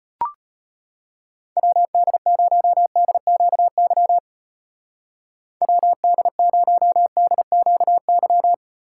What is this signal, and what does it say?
Morse code, heard as a keyed tone, about 31 wpm, 700 Hz. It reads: WD0DQY WD0DQY
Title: Speed Racing - USA 2x3 Call Signs 25wpm